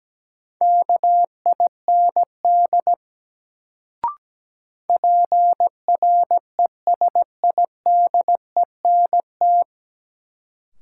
Morse code, heard as a keyed tone, about 17 wpm, 700 Hz. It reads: KIND PRESIDENT
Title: Sets of 2 Words - New Word Drill - 17wpm